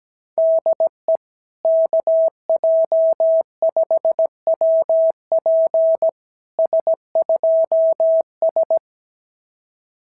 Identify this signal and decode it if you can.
Morse code, heard as a keyed tone, about 17 wpm, 660 Hz. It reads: DE KJ5WP S2S
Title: SOTA QSO Practice 17wpm